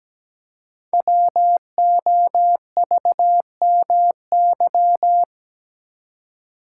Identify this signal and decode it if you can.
Morse code, heard as a keyed tone, about 17 wpm, 700 Hz. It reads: WOVMY
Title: Code Groups - 5 Letters - 17wpm